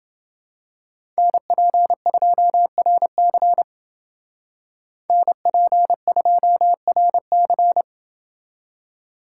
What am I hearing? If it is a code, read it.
Morse code, heard as a keyed tone, about 28 wpm, 700 Hz. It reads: NP2RC NP2RC